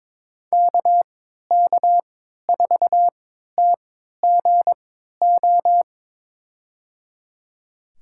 Morse code, heard as a keyed tone, about 22 wpm, spaced as for 13 wpm, 700 Hz. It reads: KK4TGO
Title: USA Call Signs - 6 Characters - 3x Character Spacing - 22wpm